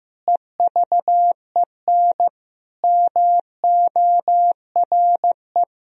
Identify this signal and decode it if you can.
Morse code, heard as a keyed tone, about 15 wpm, 700 Hz. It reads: EVEN MORE
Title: Sentences from Top 1000 Words - Review Part 3 - 15wpm